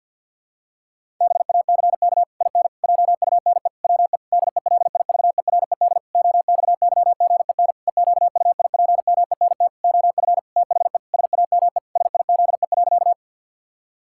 Morse code, heard as a keyed tone, about 50 wpm, 700 Hz. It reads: BACK IN JUNE WE DELIVERED OXYGEN EQUIPMENT OF THE SAME SIZE.
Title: Pangram Challenge v2 - 50wpm